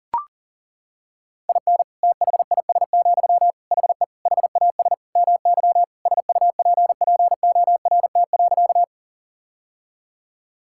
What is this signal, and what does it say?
Morse code, heard as a keyed tone, about 40 wpm, 700 Hz. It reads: IN THIS, HE HAS MY SUPPORT.
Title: Sentences from Top 1000 Words - Review Part 5 - 40wpm